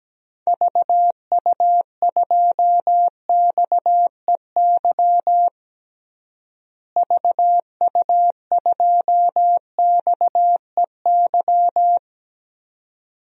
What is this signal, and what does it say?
Morse code, heard as a keyed tone, about 17 wpm, 700 Hz. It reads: VU2XEY VU2XEY
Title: DXCC-100 Call Signs - 6 Characters - Repeated 2x - 17wpm